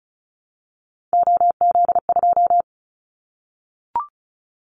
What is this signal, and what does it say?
Morse code, heard as a keyed tone, about 34 wpm, 700 Hz. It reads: OZ2
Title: Callsign Prefixes for Top 100 DXCC - 35wpm